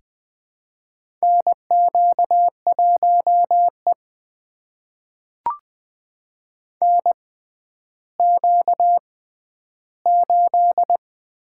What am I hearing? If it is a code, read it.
Morse code, heard as a keyed tone, about 20 wpm, spaced as for 11 wpm, 700 Hz. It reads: NQ1E NQ8
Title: USA Call Signs - 4 Characters - 6x Character Spacing - 20wpm